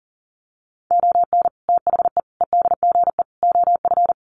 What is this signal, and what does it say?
Morse code, heard as a keyed tone, about 40 wpm, 700 Hz. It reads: ON THE EDGE OF
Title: Sets of 4 Words 40wpm